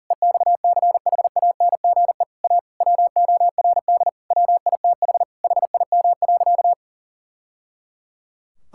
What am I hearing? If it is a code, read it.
Morse code, heard as a keyed tone, about 40 wpm, 700 Hz. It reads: EXCHANGE A WORD WITH HIM.
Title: Sentences from Top 1000 Words - Review Part 6 - 40wpm